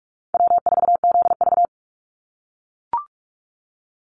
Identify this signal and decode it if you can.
Morse code, heard as a keyed tone, about 44 wpm, 700 Hz. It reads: W4ZV
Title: NA SOTA Chaser Callsigns 45wpm